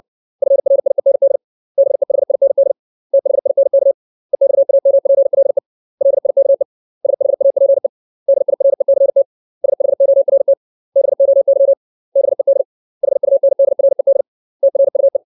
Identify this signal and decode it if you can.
Morse code, heard as a keyed tone, about 60 wpm, 550 Hz. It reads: FRIEND BEHIND THINK EXAMPLE LIKE SHAPE DIRECT SHORT BOY BED HUNDRED TRUE
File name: 632 words / 632 Words 60WPM